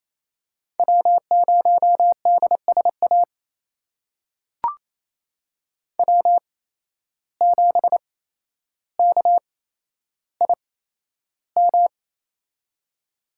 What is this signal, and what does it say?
Morse code, heard as a keyed tone, about 27 wpm, spaced as for 12 wpm, 700 Hz. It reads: W0DSA W7KIM
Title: USA Call Signs - 5 Characters - 8x Character Spacing - 28wpm